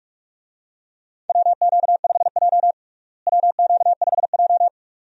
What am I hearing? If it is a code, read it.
Morse code, heard as a keyed tone, about 45 wpm, 700 Hz. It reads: WQ5J WQ5J
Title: USA Call Signs - 4 Characters - Repeated 2x - 45wpm